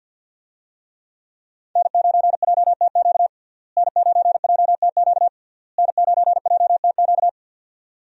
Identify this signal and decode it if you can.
Morse code, heard as a keyed tone, about 46 wpm, 700 Hz. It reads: N9JTQ N9JTQ N9JTQ
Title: USA Call Signs - 5 Characters - Repeated 3x - 50wpm